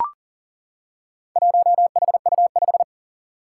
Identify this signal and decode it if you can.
Morse code, heard as a keyed tone, about 40 wpm, 700 Hz. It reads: 1HU5
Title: Four Challenging Letter-Number Combinations 40wpm